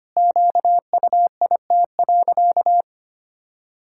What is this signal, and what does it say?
Morse code, heard as a keyed tone, about 25 wpm, 700 Hz. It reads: QUIT.
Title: Pangram Challenge v2 - 25wpm